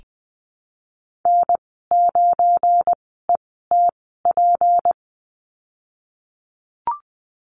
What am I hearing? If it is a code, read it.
Morse code, heard as a keyed tone, about 19 wpm, spaced as for 14 wpm, 700 Hz. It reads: N9ETP
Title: 173 Lesson 28g - Number 9 - Call Signs 20wpm